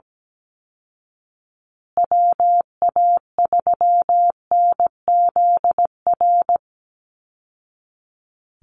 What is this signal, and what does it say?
Morse code, heard as a keyed tone, about 17 wpm, 700 Hz. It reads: WA3NZR
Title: Call Signs Repeated 3x - 17wpm